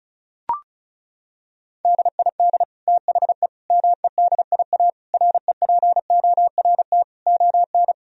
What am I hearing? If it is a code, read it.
Morse code, heard as a keyed tone, about 34 wpm, 700 Hz. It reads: DID THE MEDIA REPORT ON
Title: Sentences from Top 1000 Words - Review Part 4 - 35wpm